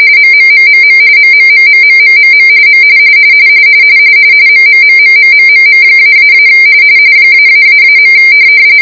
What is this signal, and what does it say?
RTTY, about 45.45 baud. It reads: MMMMM MMMMMMMM M MMMMMMMM--- ----.